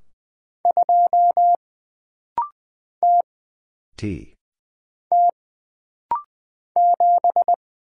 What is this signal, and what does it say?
Morse code, heard as a keyed tone, about 20 wpm, 700 Hz. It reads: 2 T T 7
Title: Single Letter-Number Alt - Rapid-Fire 20wpm